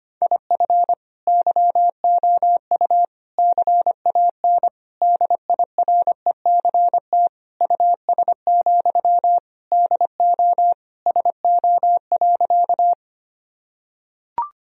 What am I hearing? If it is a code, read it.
Morse code, heard as a keyed tone, about 25 wpm, 700 Hz. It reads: IF YOU CAN DIRECT US, DO SO.
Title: Sentences from Top 1000 Words - Review Part 2 - 25wpm